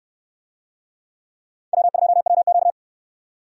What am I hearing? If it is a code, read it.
Morse code, heard as a keyed tone, about 66 wpm, 700 Hz. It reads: W2WQ